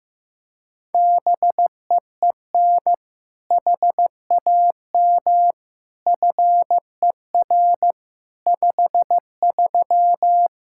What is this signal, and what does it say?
Morse code, heard as a keyed tone, about 15 wpm, 700 Hz. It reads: BEEN HAM FER 53
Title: Practice QSO - Exchange 3 - More Info - 15wpm